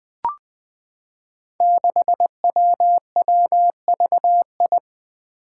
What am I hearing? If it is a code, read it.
Morse code, heard as a keyed tone, about 21 wpm, 700 Hz. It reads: 6WWVI
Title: Five Challenging Letter-Number Combinations 20wpm